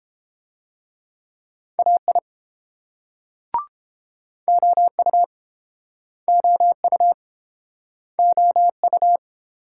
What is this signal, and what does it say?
Morse code, heard as a keyed tone, about 31 wpm, 700 Hz. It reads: AI OU OU OU